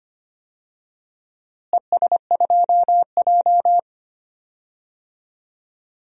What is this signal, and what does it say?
Morse code, heard as a keyed tone, about 25 wpm, 700 Hz. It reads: ES2J
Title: DXCC-100 Call Signs - 4 Characters - Repeated 2x - 25wpm